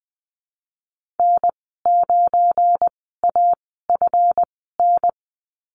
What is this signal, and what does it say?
Morse code, heard as a keyed tone, about 20 wpm, spaced as for 15 wpm, 700 Hz. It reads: N9AFN
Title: 173 Lesson 28g - Number 9 - Call Signs 20wpm